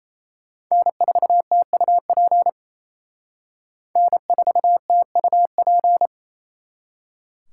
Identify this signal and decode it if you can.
Morse code, heard as a keyed tone, about 35 wpm, 700 Hz. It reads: N4TUP N4TUP